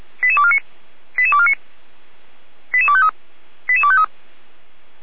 Five-tone selective call (ZVEI), heard as quick runs of five tones, tones of about 70 ms. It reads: 88249 88249 88253 88253